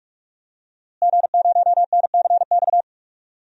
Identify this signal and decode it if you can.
Morse code, heard as a keyed tone, about 46 wpm, 700 Hz. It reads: G0NCX
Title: DXCC-100 Call Signs - 5 Characters - Repeated 3x - 45wpm